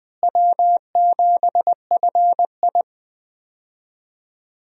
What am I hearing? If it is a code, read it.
Morse code, heard as a keyed tone, about 20 wpm, 700 Hz. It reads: W7FI
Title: USA Call Signs - 4 Characters - Repeated 3x - 20wpm